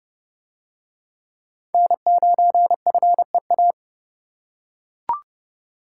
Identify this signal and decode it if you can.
Morse code, heard as a keyed tone, about 30 wpm, 700 Hz. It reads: N9FEA